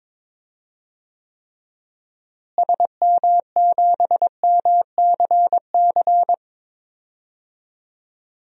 Morse code, heard as a keyed tone, about 22 wpm, 700 Hz. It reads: SM7MCC
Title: DXCC-100 Call Signs - 6 Characters - Repeated 2x - 22wpm